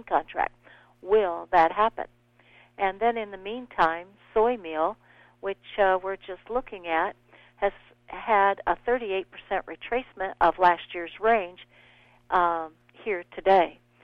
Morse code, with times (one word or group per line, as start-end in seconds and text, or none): none